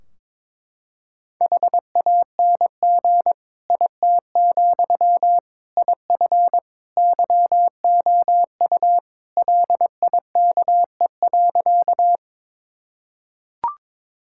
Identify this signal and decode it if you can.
Morse code, heard as a keyed tone, about 22 wpm, 700 Hz. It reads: HANG IT, IF YOU LIKE.
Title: Sentences from Top 1600 Words 22wpm